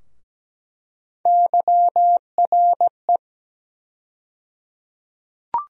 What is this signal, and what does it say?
Morse code, heard as a keyed tone, about 17 wpm, 700 Hz. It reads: YRE
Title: USA Call Sign Postfixes - 3 Letters 17wpm